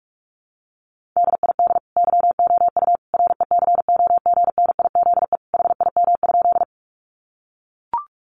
Morse code, heard as a keyed tone, about 44 wpm, 700 Hz. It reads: DID YOU RECOGNIZE HIM?
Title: Sentences from Top 1400 Words 45wpm